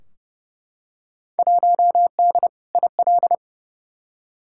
Morse code, heard as a keyed tone, about 30 wpm, 700 Hz. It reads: 1D IL
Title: Field Day Exchanges - 30wpm